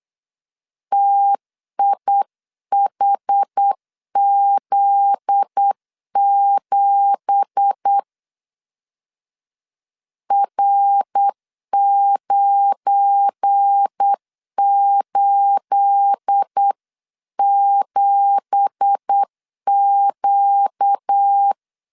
Morse code, heard as a keyed tone, about 8 wpm, 800 Hz. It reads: TIHZ7 R987Q
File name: 8wpm